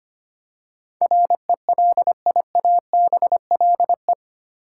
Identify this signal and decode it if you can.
Morse code, heard as a keyed tone, about 25 wpm, 700 Hz. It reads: RELIABLE
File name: Eight Letter Words - 25wpm